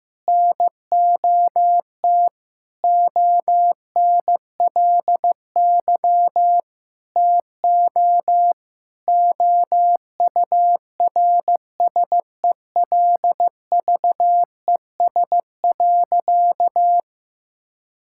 Morse code, heard as a keyed tone, about 16 wpm, 700 Hz. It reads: NOT ONLY TO OURSELVES.
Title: Sentences from Top 1600 Words 15wpm